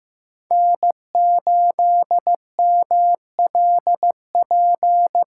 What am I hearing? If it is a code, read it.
Morse code, heard as a keyed tone, about 15 wpm, 700 Hz. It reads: N8MLP
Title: USA Call Signs - 5 Characters - 8x Character Spacing - 15wpm